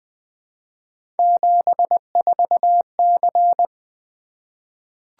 Morse code, heard as a keyed tone, about 21 wpm, 700 Hz. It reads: 74C
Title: Three Letter-Number Combinations - 20wpm